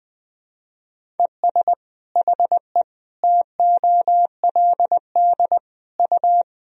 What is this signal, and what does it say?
Morse code, heard as a keyed tone, about 20 wpm, 700 Hz. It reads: ES HE TOLD U
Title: Sentences from Top 500 Words and Abbreviations 20wpm